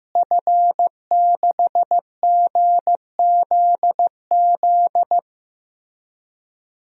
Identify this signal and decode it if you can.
Morse code, heard as a keyed tone, about 15 wpm, 700 Hz. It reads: F6GZZ
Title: DXCC-100 Call Signs - 5 Characters - Repeated 3x - 15wpm